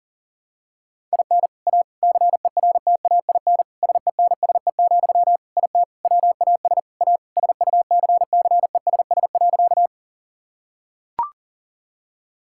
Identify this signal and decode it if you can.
Morse code, heard as a keyed tone, about 37 wpm, 700 Hz. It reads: IN A CERTAIN SENSE, IT WAS A SUCCESS.
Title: Sentences from Top 1000 Words - Review Part 6 - 40wpm